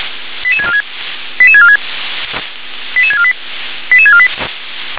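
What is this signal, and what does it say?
Five-tone selective call (ZVEI), heard as quick runs of five tones, tones of about 70 ms. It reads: 88646 88646 88648 88648
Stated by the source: Received on an SDR; ZVEI five-tone call 88149.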